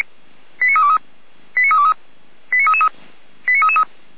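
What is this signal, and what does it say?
Five-tone selective call (ZVEI), heard as quick runs of five tones, tones of about 70 ms. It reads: 89323 89323 89333 89333
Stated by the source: Received by SDR.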